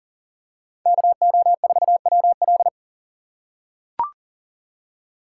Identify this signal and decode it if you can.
Morse code, heard as a keyed tone, about 38 wpm, 700 Hz. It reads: KO4WL
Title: USA Call Signs - 5 Characters - 3x Character Spacing - 40wpm